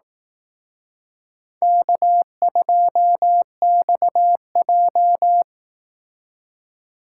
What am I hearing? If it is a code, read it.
Morse code, heard as a keyed tone, about 18 wpm, 700 Hz. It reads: K2XJ
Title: Speed Racing - USA 1x2 Call Signs 15wpm